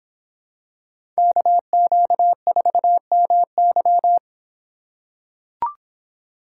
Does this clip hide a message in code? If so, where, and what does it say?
Yes: Morse code from 1.2 to 4.2 s, heard as a keyed tone, about 26 wpm, 700 Hz. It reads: KQ4MY